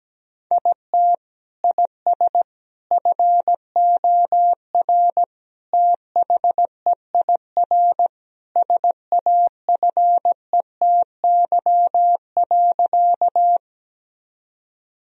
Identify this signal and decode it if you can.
Morse code, heard as a keyed tone, about 17 wpm, 700 Hz. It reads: IT IS FOR THEIR SAFETY.
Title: Sentences from Top 1000 Words - Review Part 4 - 17wpm